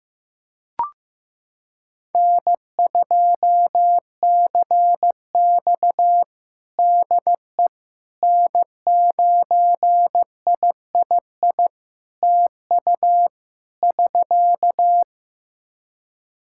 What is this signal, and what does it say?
Morse code, heard as a keyed tone, about 16 wpm, 700 Hz. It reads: N2CX DE N9III TU <SK>
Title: K1USN SST QSO Practice 15wpm